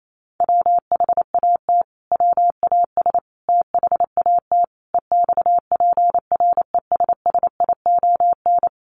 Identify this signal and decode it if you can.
Morse code, heard as a keyed tone, about 28 wpm, 700 Hz. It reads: WHAT WAS THAT EXPRESSION
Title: Sentences from Top 1200 Words 28wpm